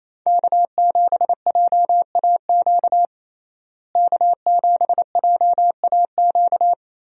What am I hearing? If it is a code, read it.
Morse code, heard as a keyed tone, about 28 wpm, 700 Hz. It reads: K7JAQ K7JAQ
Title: USA Call Signs - 5 Characters - Repeated 2x - 28wpm